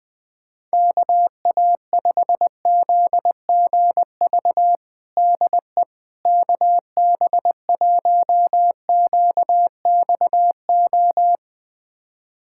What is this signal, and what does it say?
Morse code, heard as a keyed tone, about 21 wpm, 700 Hz. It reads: KA5ZGV DE KB1QXO
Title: USA Callsign DE Callsign - 6 Characters - 20wpm